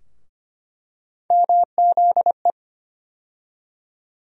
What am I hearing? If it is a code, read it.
Morse code, heard as a keyed tone, about 25 wpm, 700 Hz. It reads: MZE
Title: USA Call Sign Postfixes - 3 Letters 25wpm